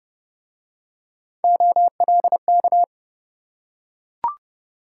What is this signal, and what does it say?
Morse code, heard as a keyed tone, about 28 wpm, 700 Hz. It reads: OLK